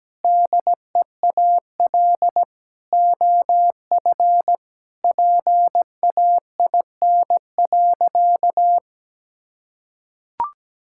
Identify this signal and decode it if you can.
Morse code, heard as a keyed tone, about 17 wpm, 700 Hz. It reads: DEAL OF PAIN.
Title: Sentences from Top 1600 Words 17wpm